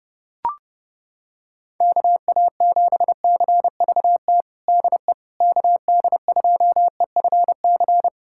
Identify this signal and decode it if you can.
Morse code, heard as a keyed tone, about 32 wpm, 700 Hz. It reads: KA7CVT DE KD2EFC
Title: USA Callsign DE Callsign 30wpm